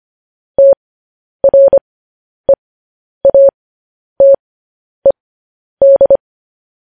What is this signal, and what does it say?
Morse code, heard as a keyed tone, about 25 wpm, spaced as for 10 wpm, 550 Hz. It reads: TREATED
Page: News Headlines in Morse Code at 10 WPM